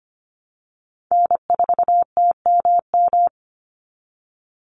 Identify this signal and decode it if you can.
Morse code, heard as a keyed tone, about 25 wpm, 700 Hz. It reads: N4TMM
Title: Call Signs Repeated 3x - 25wpm